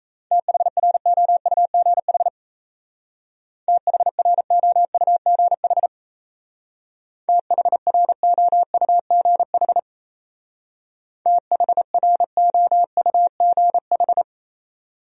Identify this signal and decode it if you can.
Morse code, heard as a keyed tone, about 34 wpm, 700 Hz. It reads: THROUGH THROUGH THROUGH THROUGH
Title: Speed-Racing Top 500 Words 28wpm